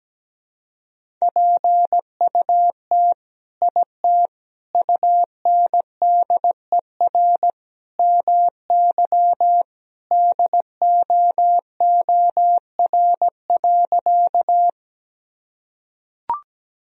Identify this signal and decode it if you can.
Morse code, heard as a keyed tone, about 17 wpm, 700 Hz. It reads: PUT IT UNDER MY DOOR.
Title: Sentences from Top 1000 Words - Review Part 1 - 17wpm